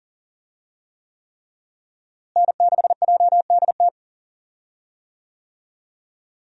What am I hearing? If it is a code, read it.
Morse code, heard as a keyed tone, about 40 wpm, 700 Hz. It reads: N6JDT
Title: USA Call Signs - 5 Characters - Repeated 3x - 40wpm